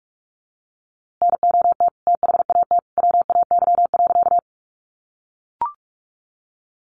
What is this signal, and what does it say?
Morse code, heard as a keyed tone, about 44 wpm, 700 Hz. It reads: NOT THAT WAY.
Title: A Sound Of Thunder 45wpm